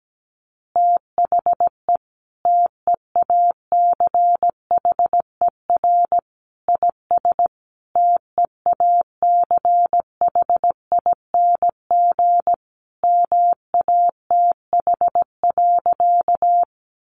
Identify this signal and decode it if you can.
Morse code, heard as a keyed tone, about 17 wpm, 700 Hz. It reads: THE TEACHER IS TEACHING MATH.
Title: Easy Phrases 17wpm